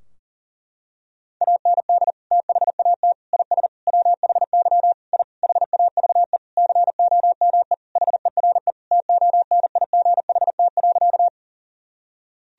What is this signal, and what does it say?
Morse code, heard as a keyed tone, about 40 wpm, 700 Hz. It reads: AND THAT IS WHY I HAVE COME HERE TONIGHT.
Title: Sentences from Top 1000 Words - Review Part 3 - 40wpm